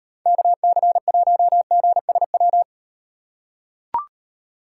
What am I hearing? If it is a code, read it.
Morse code, heard as a keyed tone, about 38 wpm, 700 Hz. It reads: KC1GSW